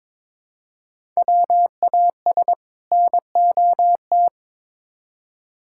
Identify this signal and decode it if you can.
Morse code, heard as a keyed tone, about 22 wpm, 700 Hz. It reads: WAS NOT
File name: practice / 22wpm / Build A Phrase - 2-Word Phrases - 22wpm